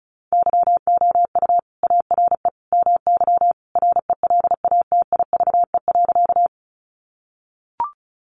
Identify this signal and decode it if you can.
Morse code, heard as a keyed tone, about 34 wpm, 700 Hz. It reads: YOU ARE MY RELATIVE.
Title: Sentences from Top 1400 Words 35wpm